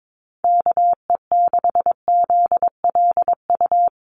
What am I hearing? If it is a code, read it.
Morse code, heard as a keyed tone, about 22 wpm, 700 Hz. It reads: KE6ZLU
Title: USA Call Signs Repeated 2x 22wpm